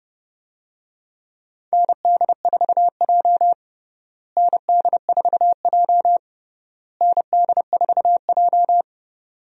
Morse code, heard as a keyed tone, about 30 wpm, 700 Hz. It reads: ND4J ND4J ND4J
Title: USA Call Signs - 4 Characters - Repeated 3x - 30wpm